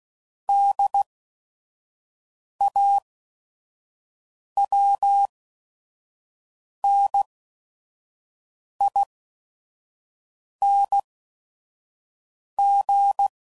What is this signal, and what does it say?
Morse code, heard as a keyed tone, about 16 wpm, spaced as for 5 wpm, 800 Hz. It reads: DAWNING
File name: Suffixes / Special ING's / VerySlowToFaster ING's